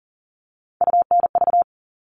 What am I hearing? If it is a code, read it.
Morse code, heard as a keyed tone, about 40 wpm, 700 Hz. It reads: UNV
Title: All Cut Numbers - 3-Digit 40wpm